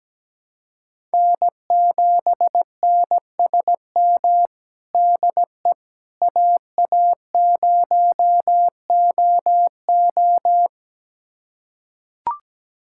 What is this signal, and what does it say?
Morse code, heard as a keyed tone, about 17 wpm, 700 Hz. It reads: N7NSM DE AA0OO
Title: USA Callsign DE Callsign - 5 Characters - 17wpm